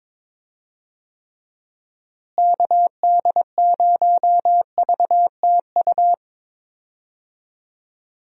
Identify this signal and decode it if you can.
Morse code, heard as a keyed tone, about 22 wpm, 700 Hz. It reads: KD0VTU
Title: USA Call Signs - 6 Characters - Repeated 3x - 22wpm